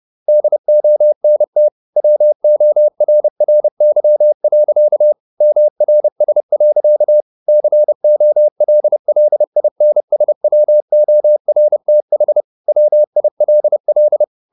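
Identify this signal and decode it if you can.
Morse code, heard as a keyed tone, about 30 wpm, 600 Hz. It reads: DONT WORRY. MRS. COLLINSWORTH WILL
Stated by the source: Morse code at 30 wpm, the novel Never by Ken Follett in English, from CW med Gnister.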